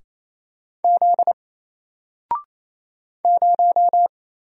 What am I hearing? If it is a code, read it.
Morse code, heard as a keyed tone, about 29 wpm, 700 Hz. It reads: Z 0
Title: Single Letter-Number 28wpm